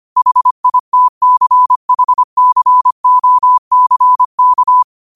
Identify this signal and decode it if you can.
Morse code, heard as a keyed tone, about 25 wpm, 1000 Hz. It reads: HITCHCOCK